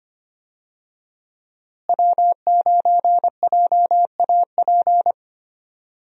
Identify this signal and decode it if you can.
Morse code, heard as a keyed tone, about 25 wpm, 700 Hz. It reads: W9JAP
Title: USA Call Signs - 5 Characters - Repeated 3x - 25wpm